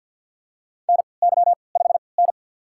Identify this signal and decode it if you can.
Morse code, heard as a keyed tone, about 47 wpm, spaced as for 28 wpm, 700 Hz. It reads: NY5N
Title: USA Call Signs - 3x Character Spacing - 50wpm